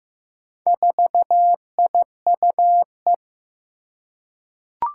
Morse code, heard as a keyed tone, about 15 wpm, 700 Hz. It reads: 4IUE
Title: Four Challenging Letter-Number Combinations 15wpm